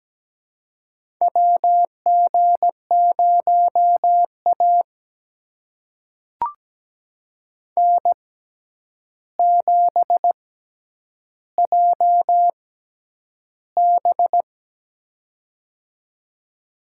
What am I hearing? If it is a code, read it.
Morse code, heard as a keyed tone, about 17 wpm, spaced as for 9 wpm, 700 Hz. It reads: WG0A N7JB
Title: USA Call Signs - 4 Characters - 6x Character Spacing - 17wpm